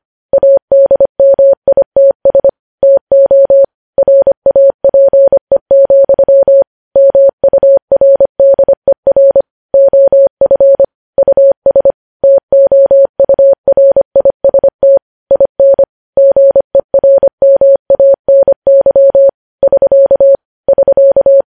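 Morse code, heard as a keyed tone, about 25 wpm, 550 Hz. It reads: ADMITS TO RAPE, MURDER OF US TOURIST IN GERMANY <SK> <SK>
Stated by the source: News headlines sent in Morse code at 25.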